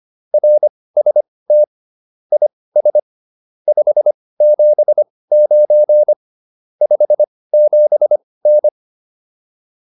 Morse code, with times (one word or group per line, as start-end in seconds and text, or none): 0.3-1.6 RST
2.3-3.0 IS
3.7-6.1 579
6.8-8.7 57N